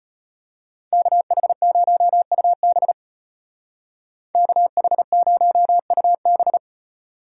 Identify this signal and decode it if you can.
Morse code, heard as a keyed tone, about 34 wpm, 700 Hz. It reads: KH0UB KH0UB